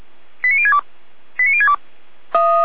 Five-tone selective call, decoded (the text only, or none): 89062 89062